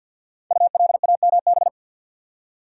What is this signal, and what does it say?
Morse code, heard as a keyed tone, about 53 wpm, 700 Hz. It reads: ULAMB
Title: Code Groups - 5 Letters - 50wpm